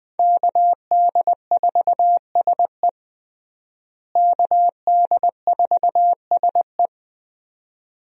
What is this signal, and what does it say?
Morse code, heard as a keyed tone, about 21 wpm, 700 Hz. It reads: KD4SE KD4SE
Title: USA Call Signs - 5 Characters - Repeated 2x - 20wpm